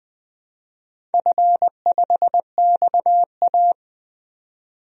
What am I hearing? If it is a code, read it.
Morse code, heard as a keyed tone, about 20 wpm, 700 Hz. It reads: F5XA
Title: DXCC-100 Call Signs - 4 Characters - 20wpm